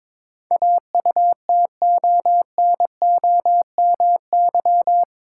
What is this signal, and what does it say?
Morse code, heard as a keyed tone, about 22 wpm, 700 Hz. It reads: AUTONOMY
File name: practice / 22wpm / Eight Letter Words - 22wpm